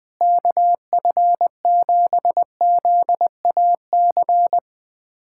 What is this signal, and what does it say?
Morse code, heard as a keyed tone, about 20 wpm, 700 Hz. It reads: KF7ZAC
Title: USA Callsign DE Callsign - 6 Characters - 20wpm